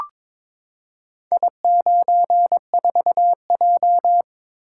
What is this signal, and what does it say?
Morse code, heard as a keyed tone, about 22 wpm, 700 Hz. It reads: I94J
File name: practice / 22wpm / Four Letter-Number Combinations - 22wpm